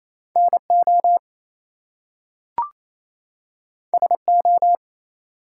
Morse code, heard as a keyed tone, about 28 wpm, 700 Hz. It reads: NO SO
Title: Two Letter Words - 28wpm